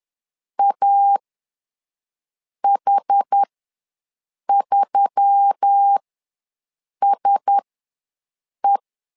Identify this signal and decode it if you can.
Morse code, heard as a keyed tone, about 10 wpm, spaced as for 6 wpm, 800 Hz. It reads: AH3SE
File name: LEZIONI / 4 Q1S3 ws 10wpm 2437